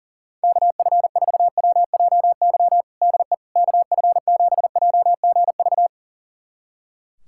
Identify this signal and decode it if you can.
Morse code, heard as a keyed tone, about 40 wpm, 700 Hz. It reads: KF4WJY DE KF7JGV
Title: USA Callsign DE Callsign 40wpm